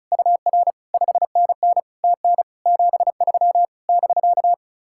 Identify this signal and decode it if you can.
Morse code, heard as a keyed tone, about 43 wpm, 700 Hz. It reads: UR 5NN TN 73 <BK>
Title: Speed-Racing Common QSO Phrases - 35wpm